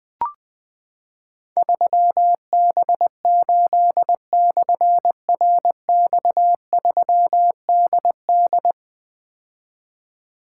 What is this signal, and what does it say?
Morse code, heard as a keyed tone, about 20 wpm, 700 Hz. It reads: 3B8/RX3DD
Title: Difficult Call Signs 20wpm